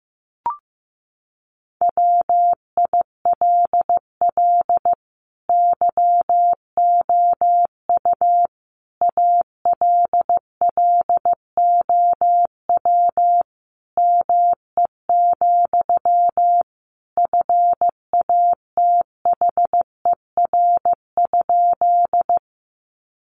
Morse code, heard as a keyed tone, about 15 wpm, 700 Hz. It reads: WILL YOU ALLOW ME, FATHER?
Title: Sentences from Top 700 Words 15wpm